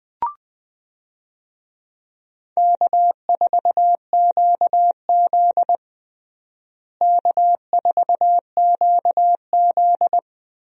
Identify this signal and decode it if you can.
Morse code, heard as a keyed tone, about 20 wpm, 700 Hz. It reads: K4QZ K4QZ
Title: USA Call Signs - 4 Characters - Repeated 3x - 20wpm